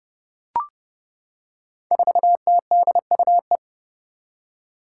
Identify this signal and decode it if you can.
Morse code, heard as a keyed tone, about 30 wpm, 700 Hz. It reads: 4TDUE
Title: Five Challenging Letter-Number Combinations 30wpm